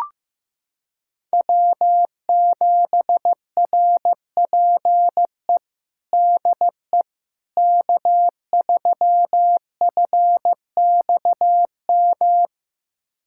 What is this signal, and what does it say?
Morse code, heard as a keyed tone, about 15 wpm, 700 Hz. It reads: W7RPE DE K3FXM
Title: USA Callsign DE Callsign - 5 Characters - 15wpm